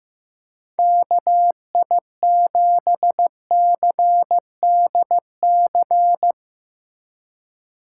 Morse code, heard as a keyed tone, about 15 wpm, 700 Hz. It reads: KI7CDC
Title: Speed Racing - USA 2x3 Call Signs 15wpm